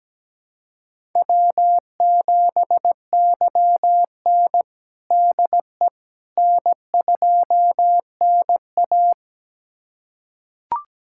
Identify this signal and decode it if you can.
Morse code, heard as a keyed tone, about 17 wpm, 700 Hz. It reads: W7YN DE N2NA
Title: USA Callsign DE Callsign - 4 Characters - 17wpm